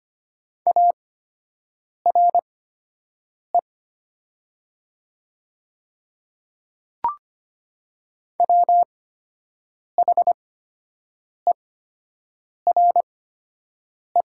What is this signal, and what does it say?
Morse code, heard as a keyed tone, about 25 wpm, spaced as for 7 wpm, 700 Hz. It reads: ARE WHERE
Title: 8x Character Spacing - Top 200 Words - 25wpm